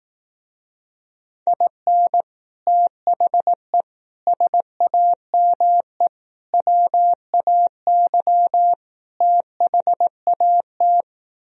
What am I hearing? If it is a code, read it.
Morse code, heard as a keyed tone, about 18 wpm, 700 Hz. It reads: IN THE SAME WAY THAT
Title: Speed-Racing - Sets of 5 Words 15wpm